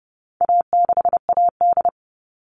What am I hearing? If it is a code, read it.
Morse code, heard as a keyed tone, about 30 wpm, 700 Hz. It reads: A6AD
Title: All Cut Numbers - 4-Digit 30wpm